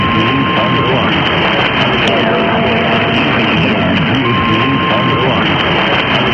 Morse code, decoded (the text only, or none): none